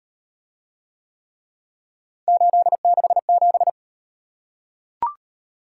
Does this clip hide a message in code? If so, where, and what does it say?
Yes: Morse code from 2.3 to 3.7 s, heard as a keyed tone, about 37 wpm, 700 Hz. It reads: 867